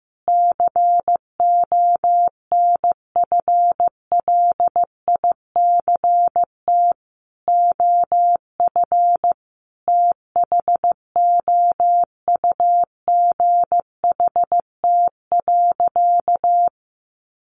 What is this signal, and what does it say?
Morse code, heard as a keyed tone, about 15 wpm, 700 Hz. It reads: CONFLICT OF THOUGHT.
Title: Sentences from Top 1300 Words 15wpm